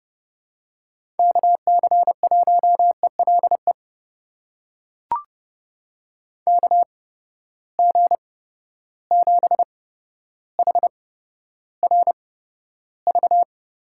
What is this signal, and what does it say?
Morse code, heard as a keyed tone, about 32 wpm, spaced as for 13 wpm, 700 Hz. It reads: KC1ELE KG7HRV
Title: USA Call Signs - 6 Characters - 8x Character Spacing - 30wpm